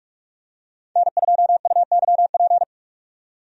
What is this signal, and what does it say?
Morse code, heard as a keyed tone, about 43 wpm, 700 Hz. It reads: N2UYP
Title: USA Call Signs - 5 Characters - 45wpm